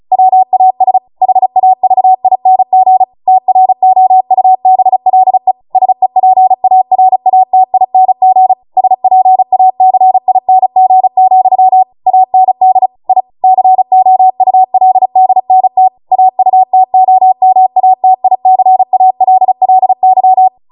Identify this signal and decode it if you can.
Morse code, heard as a keyed tone, about 36 wpm, 750 Hz. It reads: WAS HAVING TROUBLE SEPARATING SPACING, AND I COULDNT AUTOMATICALLY